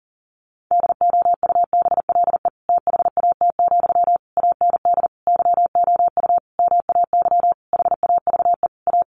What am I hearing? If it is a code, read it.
Morse code, heard as a keyed tone, about 40 wpm, 700 Hz. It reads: DOUBLE THAT, AND YOU MAY HAVE A
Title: Sentences from Top 1400 Words 40wpm